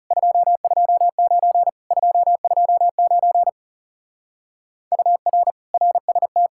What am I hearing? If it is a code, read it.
Morse code, heard as a keyed tone, about 37 wpm, 700 Hz. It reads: 229 229 UR RST
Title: Speed-Racing Common QSO Phrases - 30wpm